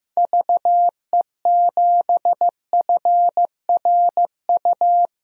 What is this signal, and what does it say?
Morse code, heard as a keyed tone, about 15 wpm, 700 Hz. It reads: VE7FRU
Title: DXCC-100 Call Signs - 6 Characters - 15wpm